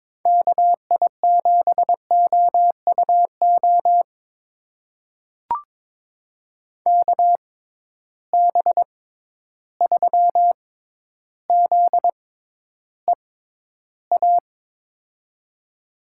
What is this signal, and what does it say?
Morse code, heard as a keyed tone, about 22 wpm, spaced as for 11 wpm, 700 Hz. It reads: KI7OUO KB3ZEA
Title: USA Call Signs - 6 Characters - 6x Character Spacing - 22wpm